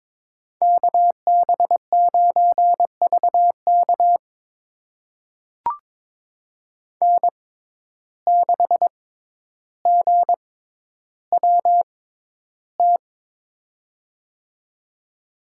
Morse code, heard as a keyed tone, about 22 wpm, spaced as for 11 wpm, 700 Hz. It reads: KB9VK N6GWT
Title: USA Call Signs - 5 Characters - 6x Character Spacing - 22wpm